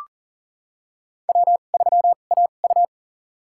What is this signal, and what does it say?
Morse code, heard as a keyed tone, about 37 wpm, spaced as for 28 wpm, 700 Hz. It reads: W3AU
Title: USA Call Signs - 2x Character Spacing - 40wpm